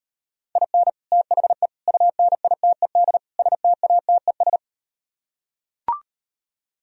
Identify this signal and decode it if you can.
Morse code, heard as a keyed tone, about 38 wpm, 700 Hz. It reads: IN THE UNITED STATES